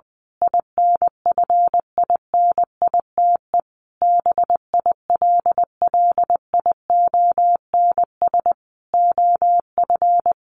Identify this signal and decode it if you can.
Morse code, heard as a keyed tone, about 19 wpm, 700 Hz. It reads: INFINITE BILLIONS OF
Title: A Sound Of Thunder 20wpm